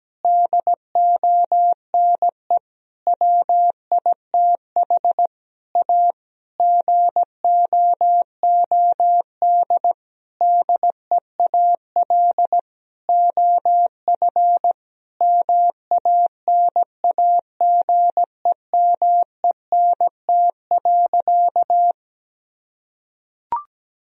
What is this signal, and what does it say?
Morse code, heard as a keyed tone, about 17 wpm, 700 Hz. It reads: DONE WITH A GOOD DEAL OF MANAGEMENT.
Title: Sentences from Top 1000 Words - Review Part 4 - 17wpm